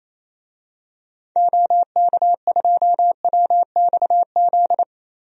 Morse code, heard as a keyed tone, about 27 wpm, 700 Hz. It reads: OK2WXZ
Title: DXCC-100 Call Signs - 6 Characters - 28wpm